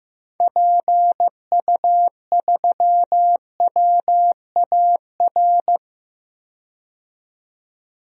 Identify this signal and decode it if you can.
Morse code, heard as a keyed tone, about 15 wpm, 700 Hz. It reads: PU3WAR
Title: DXCC-100 Call Signs - 6 Characters - 15wpm